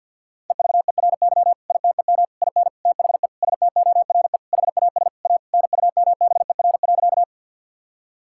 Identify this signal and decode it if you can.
Morse code, heard as a keyed tone, about 50 wpm, 700 Hz. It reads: EVERY ITEM IN THE STORE HAS A NUMBER.
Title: Sentences from Top 1000 Words - Review Part 6 - 50wpm